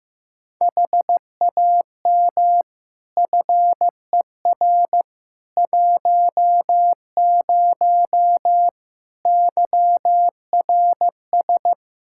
Morse code, heard as a keyed tone, about 15 wpm, 700 Hz. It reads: HAM FER 10 YRS
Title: Practice QSO - Exchange 3 - More Info - 15wpm